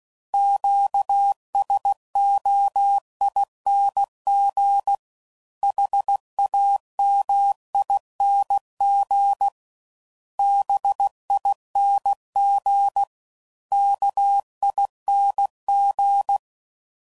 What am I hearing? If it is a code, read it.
Morse code, heard as a keyed tone, about 16 wpm, 800 Hz. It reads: QSOING HAMING BING KING
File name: Suffixes / Special INGs / Sngl Ings